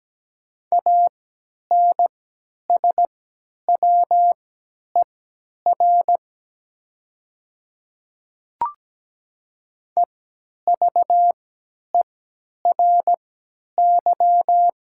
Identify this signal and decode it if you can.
Morse code, heard as a keyed tone, about 17 wpm, spaced as for 10 wpm, 700 Hz. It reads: ANSWER EVERY
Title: 3x Character Spacing - Top 300 Words - 17wpm